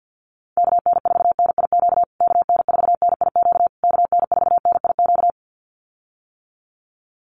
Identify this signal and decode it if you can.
Morse code, heard as a keyed tone, about 50 wpm, 700 Hz. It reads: KN4NIQ KN4NIQ KN4NIQ
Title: USA Call Signs Repeated 3x 50wpm